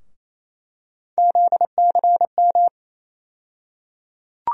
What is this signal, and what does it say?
Morse code, heard as a keyed tone, about 27 wpm, 700 Hz. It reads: ZCM
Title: USA Call Sign Postfixes - 3 Letters 28wpm